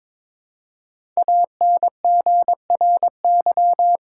Morse code, heard as a keyed tone, about 22 wpm, 700 Hz. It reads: ANGRY
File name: 5 Letter Words - First 3 Letters Given - 22wpm